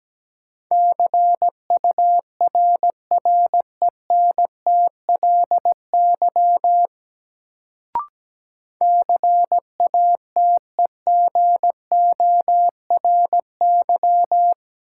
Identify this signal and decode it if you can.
Morse code, heard as a keyed tone, about 17 wpm, 700 Hz. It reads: CURRENTLY CATEGORY
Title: Sentences from Top 1000 Words - Review Part 7 - New Word Drill - 17wpm